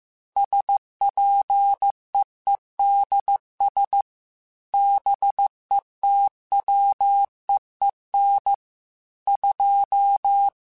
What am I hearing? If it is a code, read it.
Morse code, heard as a keyed tone, about 15 wpm, 800 Hz. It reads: SPEEDS BETWEEN 2